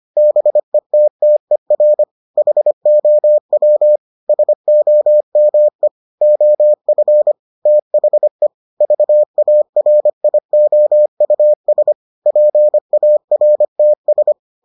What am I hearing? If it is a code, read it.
Morse code, heard as a keyed tone, about 25 wpm, 600 Hz. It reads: BETTER HOW SOME OF THE VARIOUS PARTS